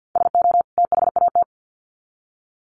Morse code, heard as a keyed tone, about 50 wpm, 700 Hz. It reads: SO THAT
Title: Sets of 2 Words 50wpm